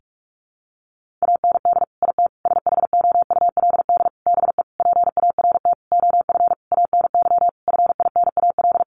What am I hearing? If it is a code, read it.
Morse code, heard as a keyed tone, about 44 wpm, 700 Hz. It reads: AND IT SHOULD BE PART OF ANY FINAL
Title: Sentences from Top 1500 Words 45wpm